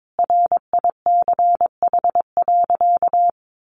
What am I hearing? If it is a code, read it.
Morse code, heard as a keyed tone, about 22 wpm, 700 Hz. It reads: RICH.
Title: Sentences from Top 1100 Words 22wpm